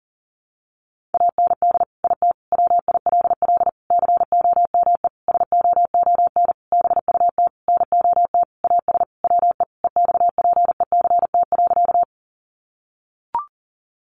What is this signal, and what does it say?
Morse code, heard as a keyed tone, about 40 wpm, 700 Hz. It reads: AND IT WILL COME SOON BUT NOT AS WE EXPECT.
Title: Sentences from Top 1100 Words 40wpm